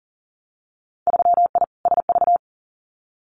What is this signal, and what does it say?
Morse code, heard as a keyed tone, about 39 wpm, 700 Hz. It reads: 3I SV
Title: Winter Field Day Exchanges Encore 40wpm